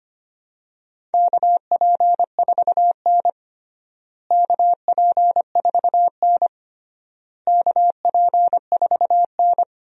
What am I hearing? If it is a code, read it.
Morse code, heard as a keyed tone, about 25 wpm, 700 Hz. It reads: KP4N KP4N KP4N
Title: DXCC-100 Call Signs - 4 Characters - Repeated 3x - 25wpm